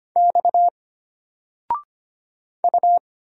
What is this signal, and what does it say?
Morse code, heard as a keyed tone, about 25 wpm, 700 Hz. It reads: X U
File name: Single-Letter-Number-25wpm